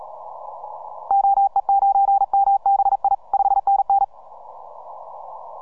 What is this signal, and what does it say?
Morse code, heard as a keyed tone, about 39 wpm, 780 Hz. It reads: OE9MBI 5NN